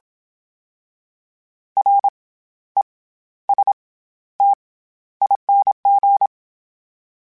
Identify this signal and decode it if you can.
Morse code, heard as a keyed tone, about 27 wpm, 800 Hz. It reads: R E S T ING
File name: Special INGs-mp3s / Special INGs-mp3s / SlowToFaster ing @25